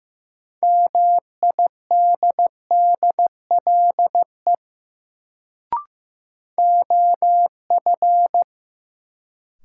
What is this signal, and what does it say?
Morse code, heard as a keyed tone, about 15 wpm, 700 Hz. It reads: MIDDLE OF
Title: Sets of 2 Words - New Word Drill - 15wpm